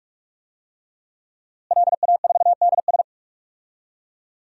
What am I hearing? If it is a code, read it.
Morse code, heard as a keyed tone, about 42 wpm, 700 Hz. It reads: LA4DS
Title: DXCC-100 Call Signs - 5 Characters - Repeated 3x - 45wpm